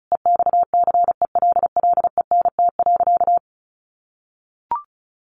Morse code, heard as a keyed tone, about 34 wpm, 700 Hz. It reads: EXCELLENT.
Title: Sentences from Top 1000 Words 35wpm